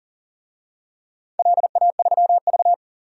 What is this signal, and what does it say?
Morse code, heard as a keyed tone, about 40 wpm, 700 Hz. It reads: LA3V
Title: DXCC-100 Call Signs - 4 Characters - 40wpm